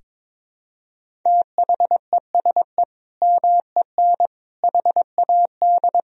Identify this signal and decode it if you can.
Morse code, heard as a keyed tone, about 22 wpm, 700 Hz. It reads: THESE MEN HAD
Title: Sentences from Top 1000 Words - Review Part 1 - 22wpm